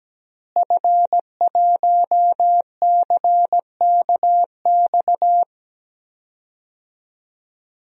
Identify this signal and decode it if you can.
Morse code, heard as a keyed tone, about 17 wpm, 700 Hz. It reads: F1CKX
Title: DXCC-100 Call Signs - 5 Characters - Repeated 2x - 17wpm